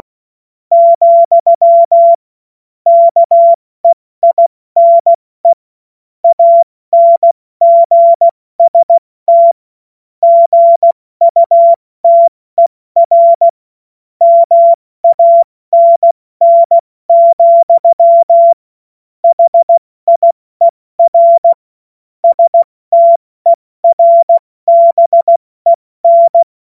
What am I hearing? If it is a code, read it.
Morse code, heard as a keyed tone, about 16 wpm, 680 Hz. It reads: , KEINE ANGST GUTER MANN, HIER STERBEN
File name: Morsekurs / Witze / Witz 22. Juni 2021, 70 F80